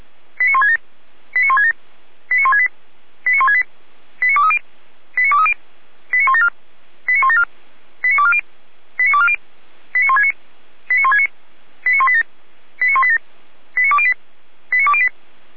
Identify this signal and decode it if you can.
Five-tone selective call (ZVEI), heard as quick runs of five tones, tones of about 70 ms. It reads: 89167 89167 89168 89168 89230 89230 89174 89174 89240 89240 89169 89169 89187 89187 89208 89208